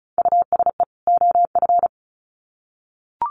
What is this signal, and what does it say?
Morse code, heard as a keyed tone, about 34 wpm, 700 Hz. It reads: USE OF
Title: Sets of 2 Words 35wpm